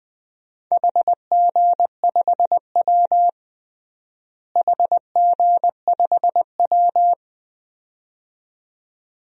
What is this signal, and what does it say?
Morse code, heard as a keyed tone, about 20 wpm, 700 Hz. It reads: HG5W HG5W
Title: DXCC-100 Call Signs - 4 Characters - Repeated 2x - 20wpm